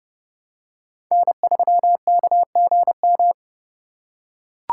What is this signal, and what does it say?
Morse code, heard as a keyed tone, about 29 wpm, 700 Hz. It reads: N3KGM